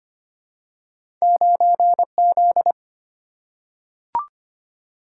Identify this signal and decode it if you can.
Morse code, heard as a keyed tone, about 25 wpm, 700 Hz. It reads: 9Z
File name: Two Letter-Number Combinations - 25wpm